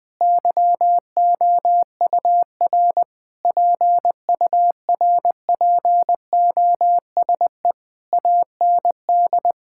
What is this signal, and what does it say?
Morse code, heard as a keyed tone, about 21 wpm, 700 Hz. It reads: YOUR PURPOSE AND
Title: Sentences with 9-Word Phrases - Spoken Beginning - 20wpm